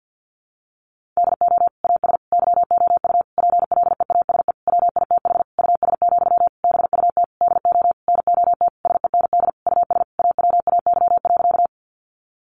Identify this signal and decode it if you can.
Morse code, heard as a keyed tone, about 50 wpm, 700 Hz. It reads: DO AS YOU PLEASE WITH US, BUT DO NOT SEND US AWAY.
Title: Sentences from Top 1500 Words 50wpm